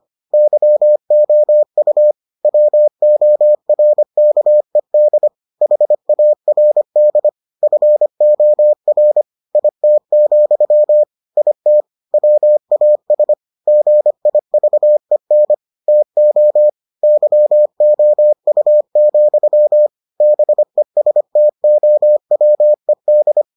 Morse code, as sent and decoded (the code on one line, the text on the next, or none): -.-- --- ..- / .-- --- .-. -.- . -.. / .... .- .-. -.. / ..-. --- .-. / .. - --..-- / .. - / .-- .- ... / --. .. ...- . -. / - --- / -.-- --- ..- --..-- / -... ... - --- .-- . -..
YOU WORKED HARD FOR IT, IT WAS GIVEN TO YOU, BESTOWED